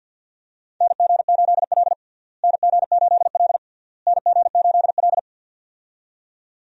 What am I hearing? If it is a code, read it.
Morse code, heard as a keyed tone, about 50 wpm, 700 Hz. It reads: NG8L NG8L NG8L
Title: USA Call Signs - 4 Characters - Repeated 3x - 50wpm